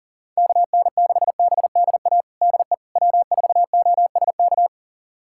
Morse code, heard as a keyed tone, about 38 wpm, 700 Hz. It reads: KN6BDA DE W4OSK
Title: USA Callsign DE Callsign 40wpm